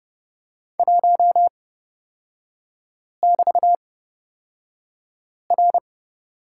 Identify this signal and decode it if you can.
Morse code, heard as a keyed tone, about 30 wpm, spaced as for 5 wpm, 700 Hz. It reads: 1<BT>R
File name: Lesson 36e - BT - Three Characters 30wpm